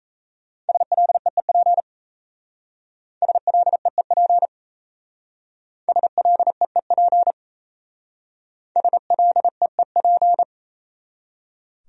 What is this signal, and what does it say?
Morse code, heard as a keyed tone, about 35 wpm, 700 Hz. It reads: SLEEP SLEEP SLEEP SLEEP
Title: Speed-Racing Top 500 Words 28wpm